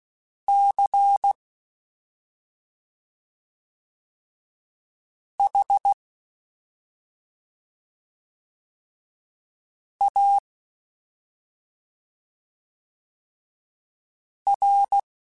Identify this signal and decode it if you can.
Morse code, heard as a keyed tone, about 16 wpm, spaced as for 2 wpm, 800 Hz. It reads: CHAR
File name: Special INGs-mp3s / Special INGs-mp3s / Slow ing @10